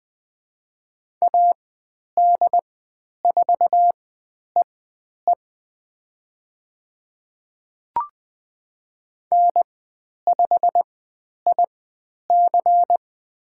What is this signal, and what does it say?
Morse code, heard as a keyed tone, about 20 wpm, spaced as for 10 wpm, 700 Hz. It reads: AD4EE N5IC
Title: 102 Lesson 17g - Letter u - Call Signs 20wpm